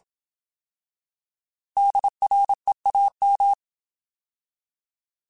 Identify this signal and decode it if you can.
Morse code, heard as a keyed tone, about 27 wpm, 800 Hz. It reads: DREAM